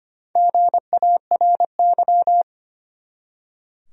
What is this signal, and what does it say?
Morse code, heard as a keyed tone, about 25 wpm, 700 Hz. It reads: GARY
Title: USA Common Names - 25wpm